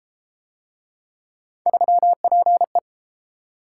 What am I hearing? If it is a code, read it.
Morse code, heard as a keyed tone, about 30 wpm, 700 Hz. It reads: 3PE